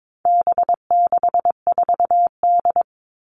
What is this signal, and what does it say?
Morse code, heard as a keyed tone, about 22 wpm, 700 Hz. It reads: B64D
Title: All Cut Numbers - 4-Digit 22wpm